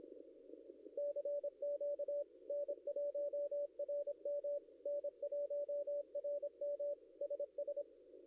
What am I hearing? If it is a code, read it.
Morse code, heard as a keyed tone, about 26 wpm, 580 Hz. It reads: CQ N1RM N1RM SS
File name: contests / ss-cw-k5tr-op-n5rz -0700